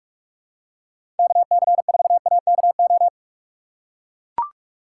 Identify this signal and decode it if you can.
Morse code, heard as a keyed tone, about 42 wpm, 700 Hz. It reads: KC4AKO
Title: USA Call Signs - 6 Characters - 6x Character Spacing - 45wpm